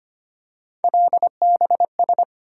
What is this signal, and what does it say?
Morse code, heard as a keyed tone, about 25 wpm, 700 Hz. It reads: LBS
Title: USA Call Sign Postfixes - 3 Letters 25wpm